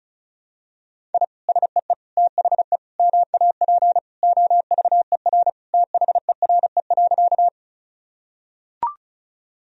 Morse code, heard as a keyed tone, about 36 wpm, 700 Hz. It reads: I SEE THE MAP OVER THERE.
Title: Sentences from Top 1000 Words - Review Part 2 - 35wpm